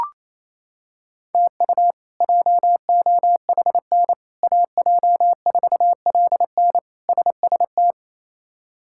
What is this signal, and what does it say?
Morse code, heard as a keyed tone, about 28 wpm, 700 Hz. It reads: TU JOHN AJ4LN SST
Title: K1USN SST QSO Practice - 28wpm